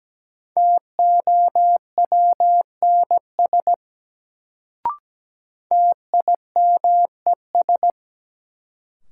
Morse code, heard as a keyed tone, about 17 wpm, 700 Hz. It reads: TOWNS TIMES
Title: Sentences from Top 1000 Words - Review Part 6 - New Word Drill - 17wpm